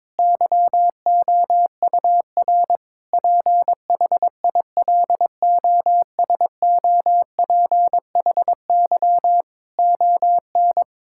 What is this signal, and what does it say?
Morse code, heard as a keyed tone, about 21 wpm, 700 Hz. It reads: YOUR PHILOSOPHY ON
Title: Sentences from Top 1000 Words - Review Part 6 - 22wpm